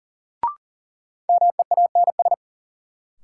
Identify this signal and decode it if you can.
Morse code, heard as a keyed tone, about 40 wpm, 700 Hz. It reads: MEANS
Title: Sets of 5 Words - New Word Drill - 40wpm